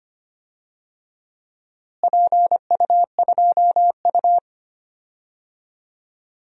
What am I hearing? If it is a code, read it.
Morse code, heard as a keyed tone, about 25 wpm, 700 Hz. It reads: PU2U
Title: DXCC-100 Call Signs - 4 Characters - Repeated 3x - 25wpm